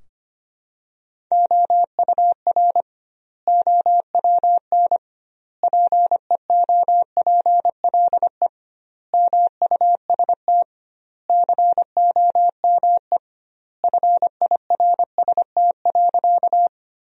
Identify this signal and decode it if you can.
Morse code, heard as a keyed tone, about 25 wpm, 700 Hz. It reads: OUR OWN PEOPLE MUST COME FIRST.
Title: 2x Word-Spacing - Sentences from Top 300 Words - 25wpm